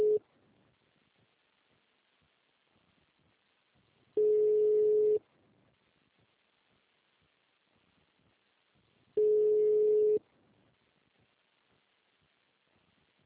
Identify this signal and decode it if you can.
Ringback tone (Europe, 425 Hz).
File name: out- -2003- - - .38